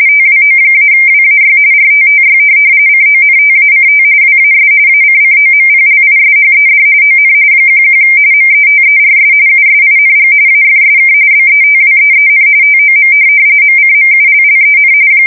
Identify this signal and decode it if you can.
RTTY, about 45.45 baud. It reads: TWO-WAY RTTY CONTACT WAS ACCOMPLISHED ON 11 METERS USING AFSK BETWEEN TOM MCMULLEN
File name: fsk170 / teletype-RTTY